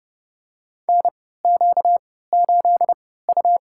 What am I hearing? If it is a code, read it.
Morse code, heard as a keyed tone, about 30 wpm, spaced as for 17 wpm, 700 Hz. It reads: NQ8U
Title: USA Call Signs - 4 Characters - 3x Character Spacing - 30wpm